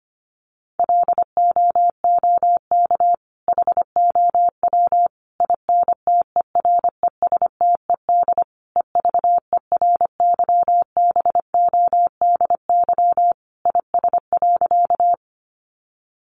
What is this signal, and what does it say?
Morse code, heard as a keyed tone, about 25 wpm, 700 Hz. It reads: LOOK HOW INTERESTED EVERYBODY IS.
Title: Sentences from Top 1400 Words 25wpm